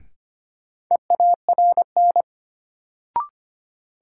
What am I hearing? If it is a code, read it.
Morse code, heard as a keyed tone, about 25 wpm, 700 Hz. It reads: EARN
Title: Sentences from Top 1500 Words - New Word Drill - 25wpm